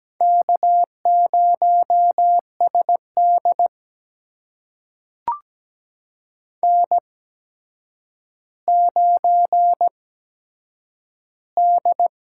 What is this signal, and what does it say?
Morse code, heard as a keyed tone, about 17 wpm, spaced as for 8 wpm, 700 Hz. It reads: K0SD N9D
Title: USA Call Signs - 4 Characters - 8x Character Spacing - 17wpm